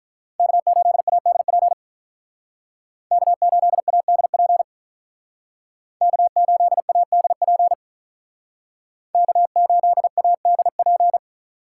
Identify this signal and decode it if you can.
Morse code, heard as a keyed tone, about 44 wpm, 700 Hz. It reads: K8ADP K8ADP K8ADP K8ADP
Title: Speed Racing - USA 1x3 Call Signs 35wpm